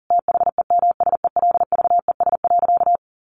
Morse code, heard as a keyed tone, about 40 wpm, 700 Hz. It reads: THEMSELVES.
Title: Sentences from Top 1300 Words 40wpm